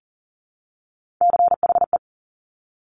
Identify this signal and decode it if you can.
Morse code, heard as a keyed tone, about 42 wpm, 700 Hz. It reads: CHE
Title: Top Three Letter Combinations in Words 40wpm